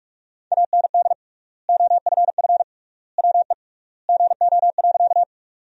Morse code, heard as a keyed tone, about 43 wpm, 700 Hz. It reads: AND OFF WE GO.
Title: 3x Word-Spacing - Sentences from Top 200 Words - 45wpm